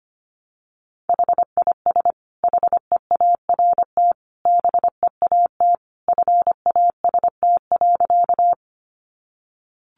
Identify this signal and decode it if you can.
Morse code, heard as a keyed tone, about 25 wpm, 700 Hz. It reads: HIS HEART BEAT FAST.
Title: Sentences from Top 1300 Words 25wpm